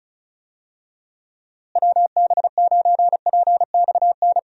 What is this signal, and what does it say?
Morse code, heard as a keyed tone, about 36 wpm, 700 Hz. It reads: WB9PXN
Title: USA Call Signs - 6 Characters - Repeated 3x - 35wpm